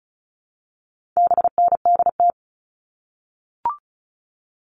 Morse code, heard as a keyed tone, about 34 wpm, 700 Hz. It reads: BNDT
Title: All Cut Numbers - 4-Digit 35wpm